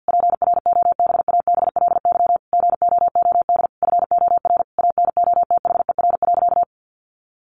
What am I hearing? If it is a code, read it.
Morse code, heard as a keyed tone, about 50 wpm, 700 Hz. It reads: PROBABLY GOOD FOR ANOTHER.
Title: Sentences from Top 1400 Words 50wpm